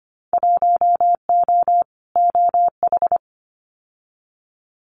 Morse code, heard as a keyed tone, about 25 wpm, 700 Hz. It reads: 1O OH
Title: Winter Field Day Exchanges 25wpm